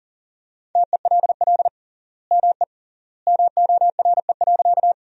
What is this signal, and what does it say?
Morse code, heard as a keyed tone, about 37 wpm, 700 Hz. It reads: TELL ME MORE.
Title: 3x Word-Spacing - Sentences from Top 300 Words - 40wpm